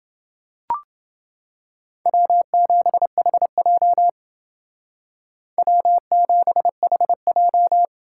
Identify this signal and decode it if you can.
Morse code, heard as a keyed tone, about 30 wpm, 700 Hz. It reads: W7HJ W7HJ